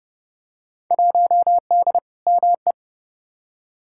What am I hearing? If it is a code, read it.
Morse code, heard as a keyed tone, about 31 wpm, 700 Hz. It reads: 1D ME
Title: Field Day Exchanges Encore - 30wpm